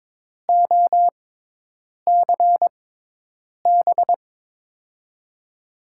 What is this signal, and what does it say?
Morse code, heard as a keyed tone, about 22 wpm, spaced as for 8 wpm, 700 Hz. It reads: OCB
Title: USA Call Signs - 6x Character Spacing - 22wpm